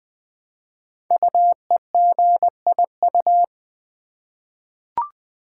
Morse code, heard as a keyed tone, about 20 wpm, 700 Hz. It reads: UEGIU
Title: Code Groups - 5 Letters - 20wpm